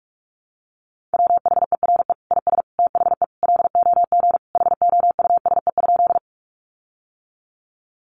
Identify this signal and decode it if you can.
Morse code, heard as a keyed tone, about 44 wpm, 700 Hz. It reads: WHERE IS THE LOG HOUSE?
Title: Sentences from Top 1400 Words 45wpm